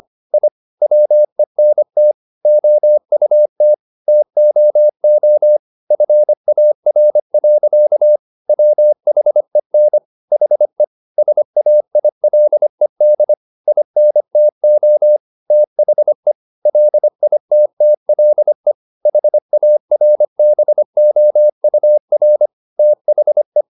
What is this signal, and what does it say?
Morse code, heard as a keyed tone, about 25 wpm, 600 Hz. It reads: I WENT OUT TOO FAR. WHEN HE SAILED INTO THE LITTLE HARBOUR THE